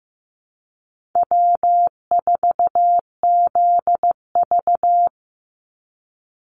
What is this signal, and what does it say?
Morse code, heard as a keyed tone, about 15 wpm, 700 Hz. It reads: W4ZV
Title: NA SOTA Chaser Callsigns 15wpm